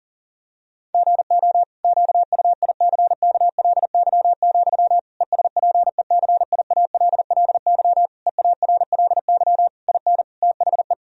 Morse code, heard as a keyed tone, about 40 wpm, 700 Hz. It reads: GO QUICKLY, ESPECIALLY EARLY IN THE
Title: Sentences from Top 1000 Words - Review Part 4 - 40wpm